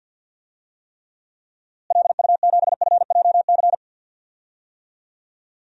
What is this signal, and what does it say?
Morse code, heard as a keyed tone, about 52 wpm, 700 Hz. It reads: LU7FJC
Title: DXCC-100 Call Signs - 6 Characters - Repeated 3x - 50wpm